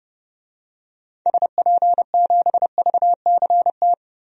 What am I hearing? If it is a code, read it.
Morse code, heard as a keyed tone, about 31 wpm, 700 Hz. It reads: SP7VCT
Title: DXCC-100 Call Signs - 6 Characters - 30wpm